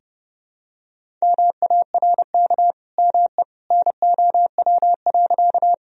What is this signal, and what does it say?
Morse code, heard as a keyed tone, about 30 wpm, 700 Hz. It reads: MARK ME NOW.
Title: Sentences from Top 1000 Words - Review Part 1 - 30wpm